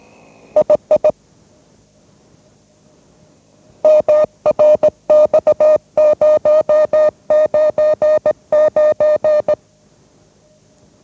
Morse code, heard as a keyed tone, about 18 wpm, 610 Hz. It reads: II MRX099
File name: mrx / mrx099 20wpm